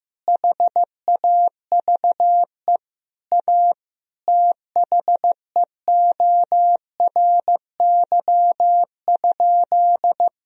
Morse code, heard as a keyed tone, about 15 wpm, 700 Hz. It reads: HAVE A THEORY?
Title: Sentences from Top 1000 Words - Review Part 3 - 15wpm